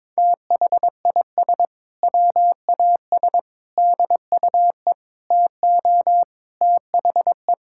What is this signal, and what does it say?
Morse code, heard as a keyed tone, about 21 wpm, 700 Hz. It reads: THIS WAS DUE TO THE
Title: Sentences from Top 1600 Words 22wpm